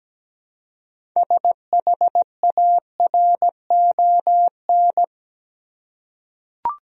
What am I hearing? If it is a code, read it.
Morse code, heard as a keyed tone, about 17 wpm, 700 Hz. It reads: SHARON
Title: USA Common Names - 17wpm